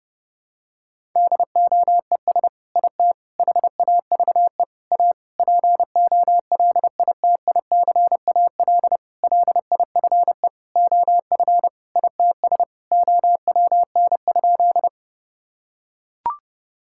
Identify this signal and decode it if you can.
Morse code, heard as a keyed tone, about 29 wpm, 700 Hz. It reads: DOES IT HAVE A POLITICAL LIFE OF ITS OWN?
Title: Sentences from Top 1000 Words - Review Part 5 - 30wpm